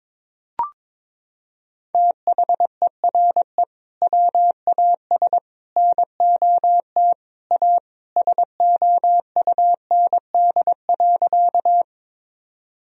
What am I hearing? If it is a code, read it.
Morse code, heard as a keyed tone, about 21 wpm, 700 Hz. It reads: THERE WAS NOT A SOUND.
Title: Sentences from Top 1000 Words - Review Part 1 - 22wpm